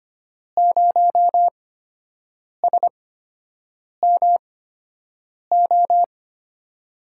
Morse code, heard as a keyed tone, about 25 wpm, spaced as for 7 wpm, 700 Hz. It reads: 0SMO
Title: USA Call Signs - 6 Characters - 8x Character Spacing - 25wpm